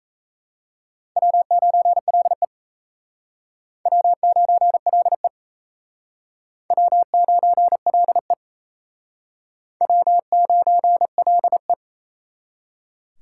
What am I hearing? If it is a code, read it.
Morse code, heard as a keyed tone, about 37 wpm, 700 Hz. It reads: W9LE W9LE W9LE W9LE